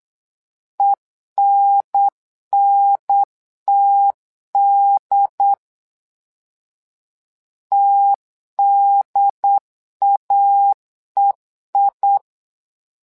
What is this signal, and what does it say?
Morse code, heard as a keyed tone, about 8 wpm, 800 Hz. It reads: ENNTD TDAEI
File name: LEZIONI / 2 UHDN / 2 UHDN 8wpm2232